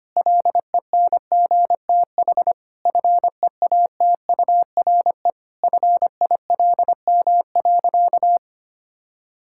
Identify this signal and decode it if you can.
Morse code, heard as a keyed tone, about 25 wpm, 700 Hz. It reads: LENGTH FEATURE FILM.
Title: Sentences from Top 1000 Words - Review Part 5 - 25wpm